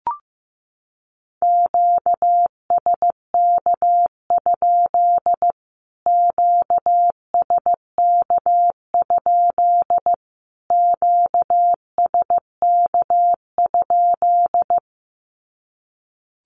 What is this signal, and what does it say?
Morse code, heard as a keyed tone, about 15 wpm, 700 Hz. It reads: QSK? QSK? QSK?
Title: Q-Codes Repeated 3x 15wpm